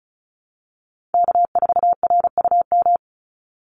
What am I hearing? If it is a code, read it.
Morse code, heard as a keyed tone, about 34 wpm, 700 Hz. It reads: K4RUM
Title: Call Signs Repeated 3x - 35wpm